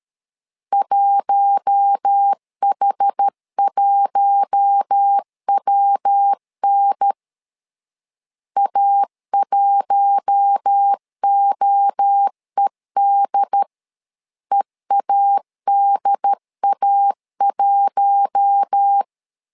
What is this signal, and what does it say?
Morse code, heard as a keyed tone, about 13 wpm, 800 Hz. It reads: 1H1WN A1OED EADA1
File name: LEZIONI / 4 Q1 12wpm 2028